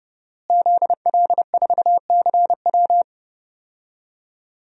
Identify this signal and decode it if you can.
Morse code, heard as a keyed tone, about 32 wpm, 700 Hz. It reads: ZL4CW
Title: DXCC-100 Call Signs - 5 Characters - Repeated 2x - 30wpm